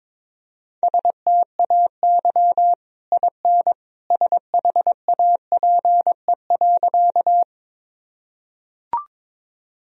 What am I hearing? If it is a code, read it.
Morse code, heard as a keyed tone, about 22 wpm, 700 Hz. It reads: STAY IN SHAPE.
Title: Sentences from Top 1000 Words - Review Part 3 - 22wpm